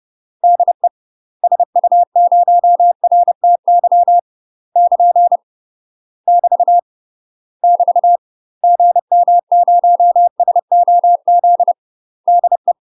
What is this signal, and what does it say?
Morse code, heard as a keyed tone, about 30 wpm, 700 Hz. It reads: DE SU0RTY <KN> <BT> <BT> GM0SOZ DE
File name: UK MASTER QSOs TRAINING / 30WPM / UK MASTER QSOs 0003